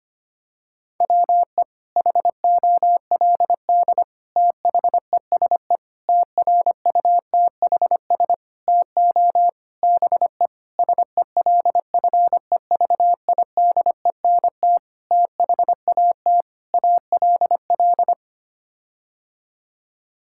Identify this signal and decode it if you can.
Morse code, heard as a keyed tone, about 25 wpm, 700 Hz. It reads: WE HOLD THESE TRUTHS TO BE SELFEVIDENT THAT ALL
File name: Sets of 9 Words - 25wpm